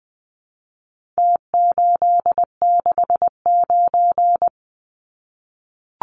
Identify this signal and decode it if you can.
Morse code, heard as a keyed tone, about 20 wpm, 700 Hz. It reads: T869
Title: Numbers - Contest Style - 4-Digit 20wpm